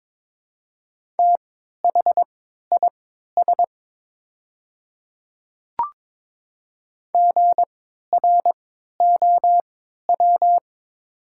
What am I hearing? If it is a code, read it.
Morse code, heard as a keyed tone, about 23 wpm, spaced as for 13 wpm, 700 Hz. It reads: THIS GROW
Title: 3x Character Spacing - Top 500 Words - 22wpm